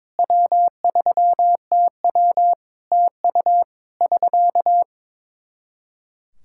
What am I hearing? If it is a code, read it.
Morse code, heard as a keyed tone, about 22 wpm, 700 Hz. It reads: W3TW TU <SK>
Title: Practice QSO - Exchange 4 - Ending - 22wpm